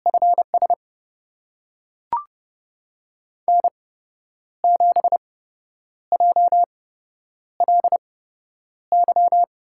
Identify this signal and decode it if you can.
Morse code, heard as a keyed tone, about 30 wpm, spaced as for 10 wpm, 700 Hz. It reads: FS N7JLY
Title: USA Call Signs - 5 Characters - 8x Character Spacing - 30wpm